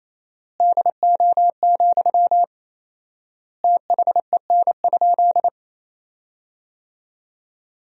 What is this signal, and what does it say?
Morse code, heard as a keyed tone, about 28 wpm, 700 Hz. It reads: DO, THEN?
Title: 4x Word-Spacing - Sentences from Top 100 Words - 28wpm